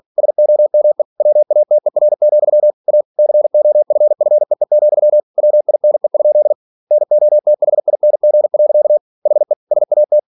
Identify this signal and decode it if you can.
Morse code, heard as a keyed tone, about 46 wpm, 600 Hz. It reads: SOME WATER, A COFFEE, WINE? NOTHING. HE SAT
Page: Never 47wpm